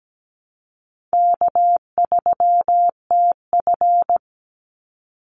Morse code, heard as a keyed tone, about 17 wpm, 700 Hz. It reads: K3TF
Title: Easy Call Signs - 17wpm